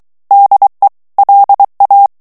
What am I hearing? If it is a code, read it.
Morse code, heard as a keyed tone, about 23 wpm, 800 Hz. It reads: DE LA
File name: morse mp3 / ESPAGNOL TEXTE EN CLAIR 2 VITESSE 22 MOTS MINUTES